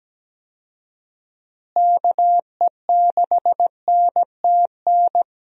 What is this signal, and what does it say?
Morse code, heard as a keyed tone, about 17 wpm, 700 Hz. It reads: KE6NTN
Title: USA Call Signs - 6 Characters - Repeated 3x - 17wpm